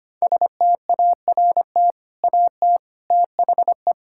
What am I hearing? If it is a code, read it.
Morse code, heard as a keyed tone, about 25 wpm, 700 Hz. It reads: START AT THE
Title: Sentences from Top 1000 Words - Review Part 6 - 25wpm